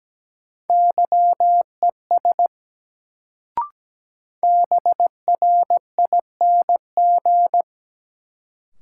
Morse code, heard as a keyed tone, about 17 wpm, 700 Hz. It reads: YES BRING
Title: Sentences from Top 500 Words - New Word Drill - 17wpm